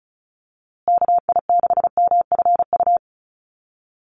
Morse code, heard as a keyed tone, about 34 wpm, 700 Hz. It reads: KI6MFU
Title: USA Call Signs Repeated 3x 35wpm